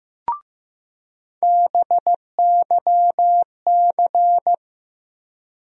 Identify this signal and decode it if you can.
Morse code, heard as a keyed tone, about 15 wpm, 700 Hz. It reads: BYC
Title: USA Call Sign Postfixes - 3 Letters 15wpm